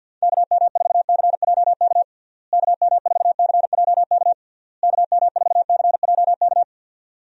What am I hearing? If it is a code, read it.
Morse code, heard as a keyed tone, about 47 wpm, 700 Hz. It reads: KM4CJK KM4CJK KM4CJK
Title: USA Call Signs - 6 Characters - Repeated 3x - 50wpm